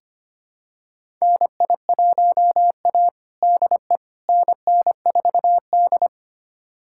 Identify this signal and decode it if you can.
Morse code, heard as a keyed tone, about 25 wpm, 700 Hz. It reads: NI1A DE NN4D
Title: USA Callsign DE Callsign - 4 Characters - 25wpm